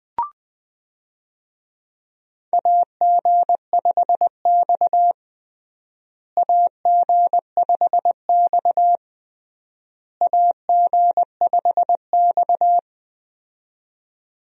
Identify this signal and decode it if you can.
Morse code, heard as a keyed tone, about 21 wpm, 700 Hz. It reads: AG5X AG5X AG5X
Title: USA Call Signs - 4 Characters - Repeated 3x - 20wpm